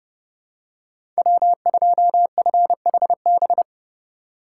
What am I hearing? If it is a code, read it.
Morse code, heard as a keyed tone, about 29 wpm, 700 Hz. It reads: W2FHB
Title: USA Call Signs - 5 Characters - 2x Character Spacing - 30wpm